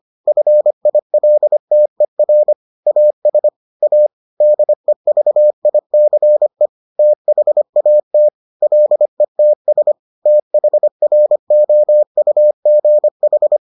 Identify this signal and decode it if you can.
Morse code, heard as a keyed tone, about 25 wpm, 600 Hz. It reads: FILTER AS A DEVICE THAT LETS THROUGH